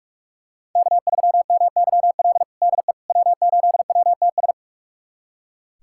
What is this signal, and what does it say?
Morse code, heard as a keyed tone, about 46 wpm, 700 Hz. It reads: K3MYL DE W8WTS
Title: USA Callsign DE Callsign - 5 Characters - 45wpm